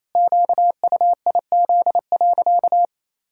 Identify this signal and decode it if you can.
Morse code, heard as a keyed tone, about 28 wpm, 700 Hz. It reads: QUIZ.
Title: Pangram Challenge v2 - 28wpm